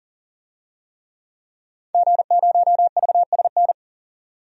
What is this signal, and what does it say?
Morse code, heard as a keyed tone, about 40 wpm, 700 Hz. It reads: G0VSN
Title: DXCC-100 Call Signs - 5 Characters - 40wpm